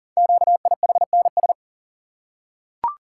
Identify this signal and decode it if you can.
Morse code, heard as a keyed tone, about 38 wpm, 690 Hz. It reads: QIHNS
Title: Code Groups - 5 Letters - 40wpm